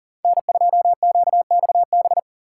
Morse code, heard as a keyed tone, about 49 wpm, 700 Hz. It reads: N2QXB
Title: Speed Racing - USA 1x3 Call Signs 40wpm